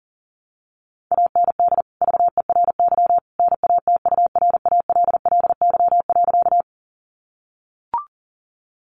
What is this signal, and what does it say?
Morse code, heard as a keyed tone, about 40 wpm, 700 Hz. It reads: AND VERY NATURALLY.
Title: Sentences from Top 1300 Words 40wpm